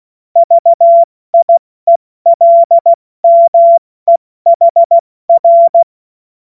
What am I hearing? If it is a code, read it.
Morse code, heard as a keyed tone, about 16 wpm, spaced as for 14 wpm, 680 Hz. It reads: VIELMEHR
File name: Morsekurs / Witze / Witz 07. Dezember 2021, 70 F80